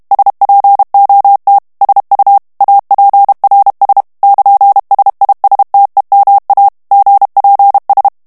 Morse code, heard as a keyed tone, about 32 wpm, 800 Hz. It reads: SPOT SU APRS <KN>SISTEMA GPS
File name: ITALIEN texte clair 1 vitesse 30 mots minute